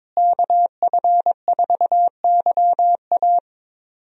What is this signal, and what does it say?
Morse code, heard as a keyed tone, about 22 wpm, 700 Hz. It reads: KF4YA
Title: USA Call Signs - 5 Characters - Repeated 3x - 22wpm